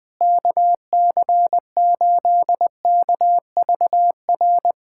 Morse code, heard as a keyed tone, about 20 wpm, 700 Hz. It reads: KC8KVR
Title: USA Callsign DE Callsign - 6 Characters - 20wpm